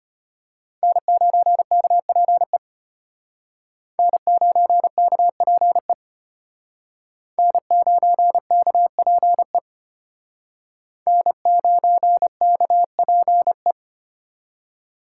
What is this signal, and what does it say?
Morse code, heard as a keyed tone, about 32 wpm, 700 Hz. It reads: N9KPE N9KPE N9KPE N9KPE